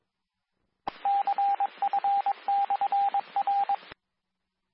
Morse code, heard as a keyed tone, about 22 wpm, 800 Hz, with radio noise. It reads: CF/R
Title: Nassau and Suffolk County Amateur Repeaters